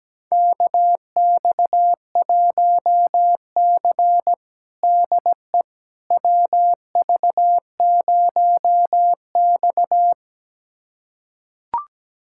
Morse code, heard as a keyed tone, about 17 wpm, 700 Hz. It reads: KX1C DE WV0X
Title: USA Callsign DE Callsign - 4 Characters - 17wpm